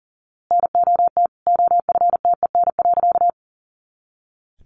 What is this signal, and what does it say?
Morse code, heard as a keyed tone, about 40 wpm, 700 Hz. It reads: NOT OFTEN.
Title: A Sound Of Thunder 40wpm